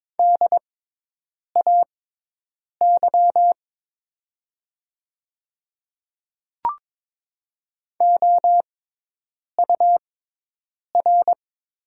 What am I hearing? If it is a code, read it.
Morse code, heard as a keyed tone, about 22 wpm, spaced as for 8 wpm, 700 Hz. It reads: DAY OUR
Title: 6x Character Spacing - Top 200 Words - 22wpm